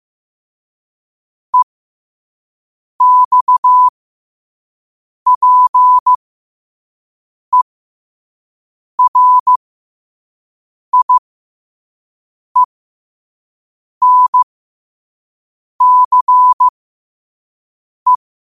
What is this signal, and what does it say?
Morse code, heard as a keyed tone, about 15 wpm, spaced as for 5 wpm, 1000 Hz. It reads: EXPERIENCE
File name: qotdfiles / 5wpm